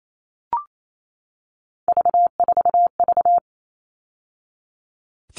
Morse code, heard as a keyed tone, about 28 wpm, 700 Hz. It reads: V4V
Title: All Cut Numbers - 3-Digit 28wpm